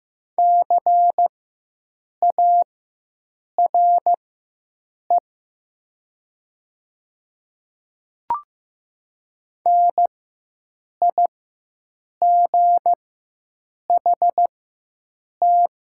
Morse code, heard as a keyed tone, about 15 wpm, spaced as for 7 wpm, 700 Hz. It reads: CARE NIGHT
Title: 4x Character Spacing - Top 300 Words - 15wpm